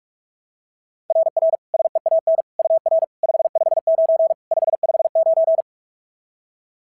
Morse code, heard as a keyed tone, about 44 wpm, 690 Hz. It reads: RR SEAN UR 559 559
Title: SOTA QSO Practice 45wpm